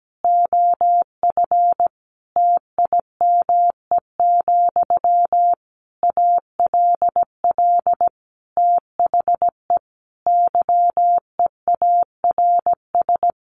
Morse code, heard as a keyed tone, about 17 wpm, 700 Hz. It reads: OF TIME, ALL THE YEARS
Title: A Sound Of Thunder 17wpm